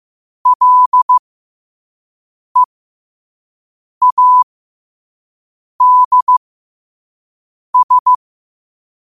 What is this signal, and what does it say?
Morse code, heard as a keyed tone, about 15 wpm, spaced as for 5 wpm, 1000 Hz. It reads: LEADS
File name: qotdfiles / 5wpm